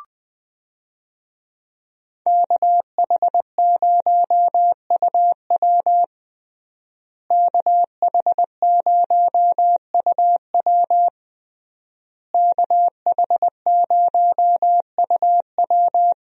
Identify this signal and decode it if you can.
Morse code, heard as a keyed tone, about 20 wpm, 700 Hz. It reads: KH0UW KH0UW KH0UW
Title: USA Call Signs - 5 Characters - Repeated 3x - 20wpm